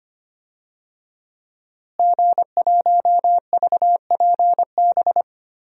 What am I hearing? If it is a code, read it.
Morse code, heard as a keyed tone, about 25 wpm, 700 Hz. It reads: G1VPB
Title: DXCC-100 Call Signs - 5 Characters - Repeated 3x - 25wpm